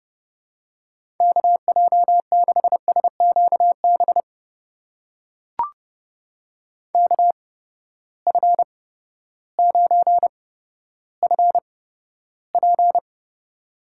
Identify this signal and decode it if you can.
Morse code, heard as a keyed tone, about 32 wpm, spaced as for 14 wpm, 700 Hz. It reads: KJ6SQB KF9FP
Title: USA Call Signs - 8x Character Spacing - 30wpm